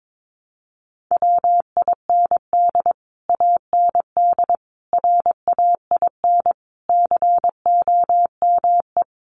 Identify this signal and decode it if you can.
Morse code, heard as a keyed tone, about 22 wpm, 700 Hz. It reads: WIND AND RAIN COME
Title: Sentences from Top 1400 Words 22wpm